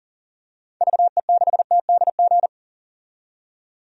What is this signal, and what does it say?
Morse code, heard as a keyed tone, about 40 wpm, 700 Hz. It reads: VE6TDG
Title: DXCC-100 Call Signs - 6 Characters - 40wpm